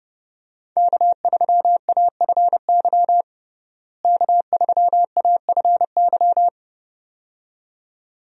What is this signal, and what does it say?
Morse code, heard as a keyed tone, about 30 wpm, 700 Hz. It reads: K3AFY K3AFY
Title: USA Call Signs - 5 Characters - Repeated 2x - 30wpm